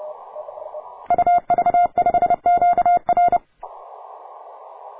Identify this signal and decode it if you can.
Morse code, heard as a keyed tone, about 30 wpm, 700 Hz. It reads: UV5QR